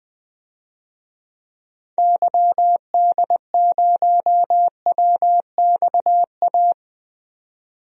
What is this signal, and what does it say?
Morse code, heard as a keyed tone, about 20 wpm, 700 Hz. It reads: YD0WXA
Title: DXCC-100 Call Signs - 6 Characters - 20wpm